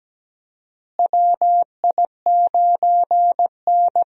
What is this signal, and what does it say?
Morse code, heard as a keyed tone, about 17 wpm, 700 Hz. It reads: WI9N
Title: USA Call Signs - 4 Characters - Repeated 3x - 17wpm